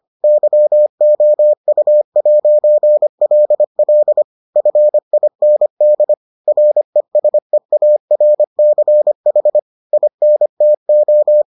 Morse code, heard as a keyed tone, about 25 wpm, 600 Hz. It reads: YOU'LL FIND RESEARCH INTO